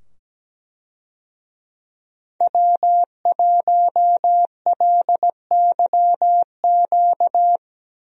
Morse code, heard as a keyed tone, about 17 wpm, 700 Hz. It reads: W1LYQ
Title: USA Call Signs - 5 Characters - 17wpm